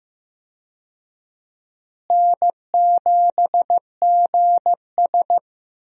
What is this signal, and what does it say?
Morse code, heard as a keyed tone, about 15 wpm, 700 Hz. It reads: N7GS
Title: USA Call Signs - 4 Characters - Repeated 2x - 15wpm